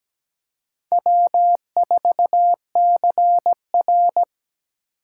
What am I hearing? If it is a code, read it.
Morse code, heard as a keyed tone, about 17 wpm, 700 Hz. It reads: W4CR
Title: USA Call Signs - 4 Characters - 4x Character Spacing - 17wpm